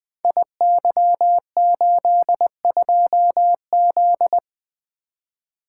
Morse code, heard as a keyed tone, about 20 wpm, 700 Hz. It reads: IY82Z
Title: Code Groups - 5 Letter:Numbers - 20wpm